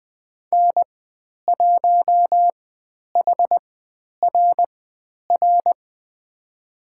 Morse code, heard as a keyed tone, about 20 wpm, spaced as for 10 wpm, 700 Hz. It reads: N1HRR
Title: 109 Lesson 18g - Letter m - Call Signs 20wpm